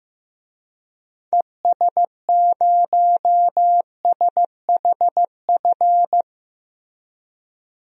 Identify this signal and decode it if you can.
Morse code, heard as a keyed tone, about 15 wpm, 700 Hz. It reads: ES0SHF
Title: DXCC-100 Call Signs - 6 Characters - Repeated 3x - 15wpm